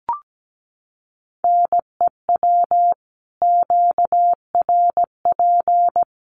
Morse code, heard as a keyed tone, about 17 wpm, 700 Hz. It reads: NEW QRP
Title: Common QSO Phrases 17wpm